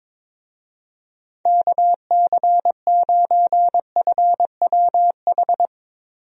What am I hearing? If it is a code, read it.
Morse code, heard as a keyed tone, about 22 wpm, 700 Hz. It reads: KC9FWH
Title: USA Call Signs - 6 Characters - Repeated 2x - 22wpm